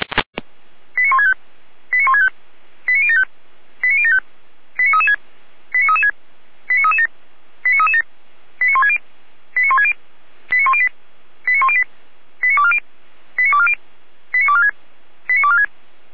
Five-tone selective call (ZVEI), heard as quick runs of five tones, tones of about 70 ms. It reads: 89165 89165 89075 89075 89336 89336 89337 89337 89160 89160 89108 89108 89240 89240 89246 89246
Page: SDR/ZVEI